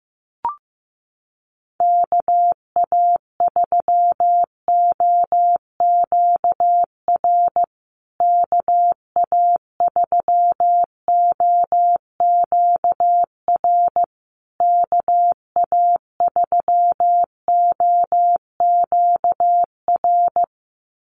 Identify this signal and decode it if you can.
Morse code, heard as a keyed tone, about 15 wpm, 700 Hz. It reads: KA3OQR KA3OQR KA3OQR
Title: USA Call Signs Repeated 3x 15wpm